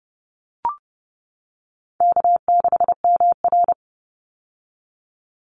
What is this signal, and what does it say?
Morse code, heard as a keyed tone, about 30 wpm, 700 Hz. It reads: K6MR
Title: Easy Call Signs - 30wpm